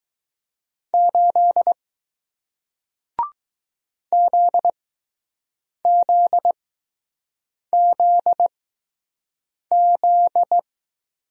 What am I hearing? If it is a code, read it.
Morse code, heard as a keyed tone, about 20 wpm, spaced as for 6 wpm, 700 Hz. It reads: 8 ZZZZ